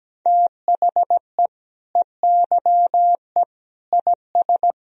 Morse code, heard as a keyed tone, about 17 wpm, 700 Hz. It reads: THE EYE IS
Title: Sentences from Top 1000 Words - Review Part 3 - 17wpm